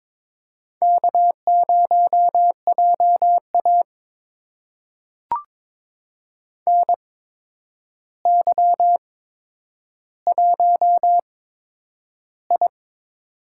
Morse code, heard as a keyed tone, about 22 wpm, spaced as for 9 wpm, 700 Hz. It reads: K0JA NY1I
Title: USA Call Signs - 4 Characters - 8x Character Spacing - 22wpm